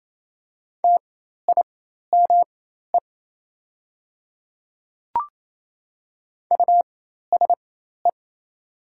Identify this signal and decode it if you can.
Morse code, heard as a keyed tone, about 28 wpm, spaced as for 13 wpm, 700 Hz. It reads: TIME USE
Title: 4x Character Spacing - Top 100 Words - 28wpm